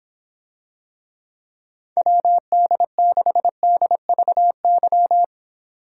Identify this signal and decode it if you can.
Morse code, heard as a keyed tone, about 26 wpm, 700 Hz. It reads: WD6DVY